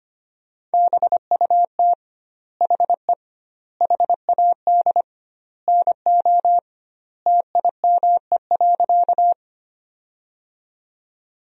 Morse code, heard as a keyed tone, about 25 wpm, 700 Hz. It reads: BUT HE HAD NO TIME.
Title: 2x Word-Spacing - Sentences from Top 100 Words - 25wpm